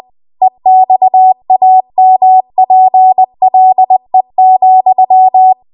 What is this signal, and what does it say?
Morse code, heard as a keyed tone, about 20 wpm, 750 Hz. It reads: EXAMPLE,
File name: Archive / 20 WPM / 20WPM